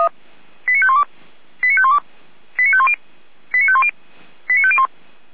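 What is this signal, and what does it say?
Five-tone selective call (ZVEI), heard as quick runs of five tones, tones of about 70 ms. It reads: 89512 89512 89510 89510 89501
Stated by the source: ZVEI five-tone call received by SDR, fire brigade Orlamünde (feuerwehr-orlamuende).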